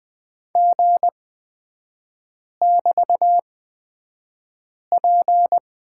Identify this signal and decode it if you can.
Morse code, heard as a keyed tone, about 20 wpm, spaced as for 5 wpm, 700 Hz. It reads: G<BT>P
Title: 224 Lesson 36e - BT - Three Characters 20wpm